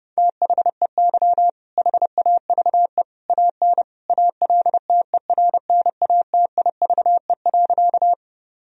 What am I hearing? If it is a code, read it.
Morse code, heard as a keyed tone, about 30 wpm, 700 Hz. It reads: THEY HAVE AN ALTERNATIVE.
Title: Sentences from Top 1000 Words - Review Part 4 - 30wpm